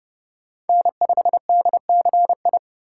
Morse code, heard as a keyed tone, about 29 wpm, 700 Hz. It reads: N5DCI